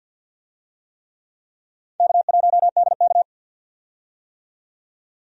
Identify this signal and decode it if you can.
Morse code, heard as a keyed tone, about 51 wpm, 700 Hz. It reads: K1DK
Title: USA Call Signs - 4 Characters - 50wpm